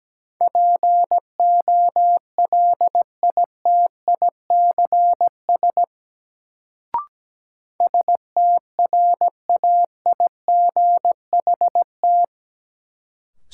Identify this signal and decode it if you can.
Morse code, heard as a keyed tone, about 17 wpm, 700 Hz. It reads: POLITICS STRAIGHT
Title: Sentences from Top 1000 Words - New Word Drill - 17wpm